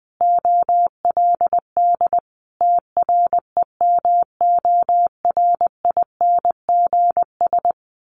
Morse code, heard as a keyed tone, about 20 wpm, 700 Hz. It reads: OLD TREMORINGS
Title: A Sound Of Thunder 20wpm